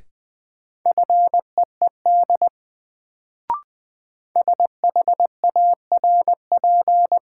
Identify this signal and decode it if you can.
Morse code, heard as a keyed tone, about 20 wpm, 700 Hz. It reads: FEED SHARP
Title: Sentences from Top 1600 Words - New Word Drill - 20wpm